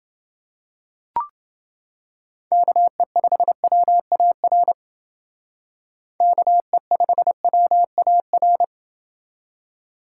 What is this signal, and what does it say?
Morse code, heard as a keyed tone, about 28 wpm, 700 Hz. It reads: KE5WAR KE5WAR